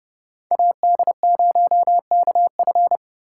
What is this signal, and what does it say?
Morse code, heard as a keyed tone, about 29 wpm, 700 Hz. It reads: AD0KF